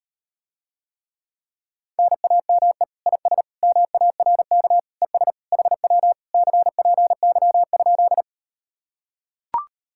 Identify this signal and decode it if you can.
Morse code, heard as a keyed tone, about 38 wpm, 700 Hz. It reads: NAME IS MARK ES HW CPY?